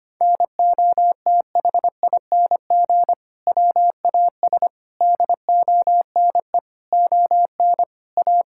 Morse code, heard as a keyed tone, about 25 wpm, 700 Hz. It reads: NOTHING WAS DONE ON A
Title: Sentences from Top 1000 Words - Review Part 5 - 25wpm